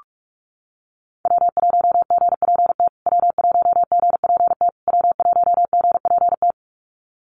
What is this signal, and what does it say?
Morse code, heard as a keyed tone, about 44 wpm, 700 Hz. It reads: W1GPT W1GPT W1GPT
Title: USA Call Signs Repeated 3x 45wpm